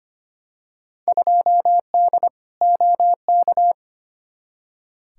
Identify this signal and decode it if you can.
Morse code, heard as a keyed tone, about 25 wpm, 700 Hz. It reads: 2D OK
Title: Field Day Exchanges Encore - 25wpm